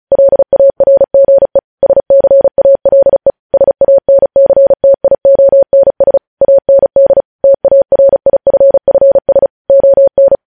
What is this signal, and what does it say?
Morse code, heard as a keyed tone, about 36 wpm, 550 Hz. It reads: LARGE SCALE SANCTIONS AND TARIFFS ON
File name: podcast / News headlines-1x-35x35